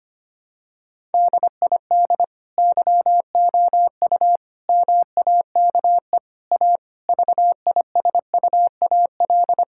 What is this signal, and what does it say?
Morse code, heard as a keyed tone, about 25 wpm, 700 Hz. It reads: DID YOU MAKE A VISUAL
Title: Sentences from Top 1000 Words - Review Part 6 - 25wpm